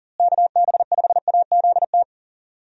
Morse code, heard as a keyed tone, about 40 wpm, 700 Hz. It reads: KB5AZT
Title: USA Call Signs - 6 Characters - Repeated 3x - 40wpm